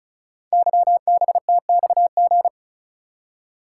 Morse code, heard as a keyed tone, about 36 wpm, 700 Hz. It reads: YBTXG
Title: Code Groups - 5 Letters - 35wpm